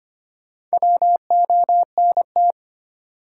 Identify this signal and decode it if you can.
Morse code, heard as a keyed tone, about 25 wpm, 700 Hz. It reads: WONT
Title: Sentences from Top 1000 Words - Review Part 6 - New Word Drill - 25wpm